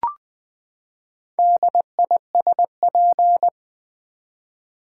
Morse code, heard as a keyed tone, about 20 wpm, 700 Hz. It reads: DISP
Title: Build A Word - 4 Letters At A Time - 8-Letter Words - 20wpm